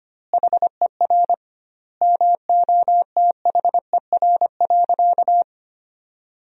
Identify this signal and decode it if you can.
Morse code, heard as a keyed tone, about 25 wpm, 700 Hz. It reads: HER MOTHER.
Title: 2x Word-Spacing - Sentences from Top 400 Words - 25wpm